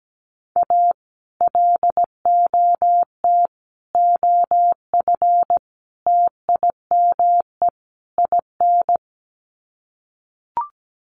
Morse code, heard as a keyed tone, about 17 wpm, 700 Hz. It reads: A LOT OF TIME IN
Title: Sets of 5 Words Encore2 17wpm